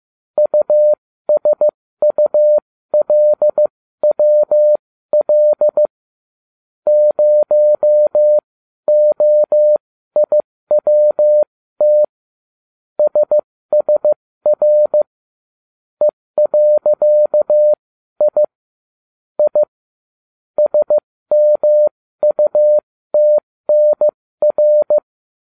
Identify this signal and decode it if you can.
Morse code, heard as a keyed tone, about 15 wpm, spaced as for 13 wpm, 600 Hz. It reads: USULWL 0OIWT SSR E.I I SMUTNR